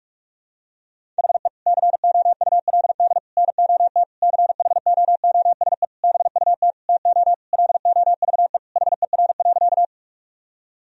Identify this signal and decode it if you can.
Morse code, heard as a keyed tone, about 46 wpm, 700 Hz. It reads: HE COULD NOT CHOOSE BUT TO LOVE HER.
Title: Sentences from Top 1000 Words - Review Part 4 - 45wpm